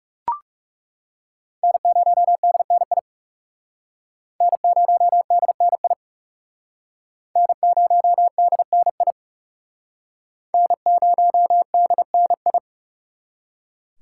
Morse code, heard as a keyed tone, about 37 wpm, 700 Hz. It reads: N0DNI N0DNI N0DNI N0DNI
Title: Speed Racing - USA 1x3 Call Signs 30wpm